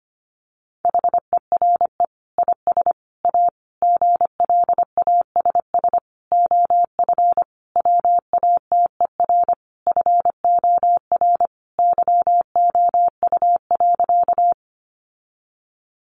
Morse code, heard as a keyed tone, about 25 wpm, 700 Hz. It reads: HERE IS A GLASS OF WATER FOR YOU.
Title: Sentences from Top 1300 Words 25wpm